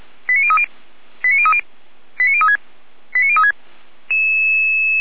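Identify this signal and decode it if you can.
Five-tone selective call (ZVEI), heard as quick runs of five tones, tones of about 70 ms. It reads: 89030 89030 89036 89036
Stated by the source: ZVEI five-tone selective call received on an SDR, recorded by Feuerwehr Orlamünde.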